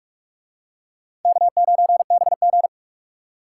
Morse code, heard as a keyed tone, about 44 wpm, 700 Hz. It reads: K9BG